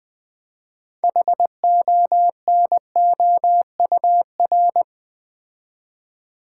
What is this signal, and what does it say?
Morse code, heard as a keyed tone, about 20 wpm, 700 Hz. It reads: HONOUR
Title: Build A Word - 3 Letters At A Time - 6-Letter Words - 20wpm